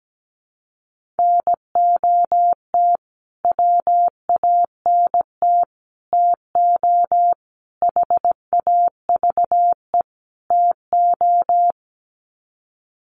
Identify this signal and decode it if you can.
Morse code, heard as a keyed tone, about 17 wpm, 700 Hz. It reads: NOT WANT TO HAVE TO
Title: Sets of 5 Words Encore2 17wpm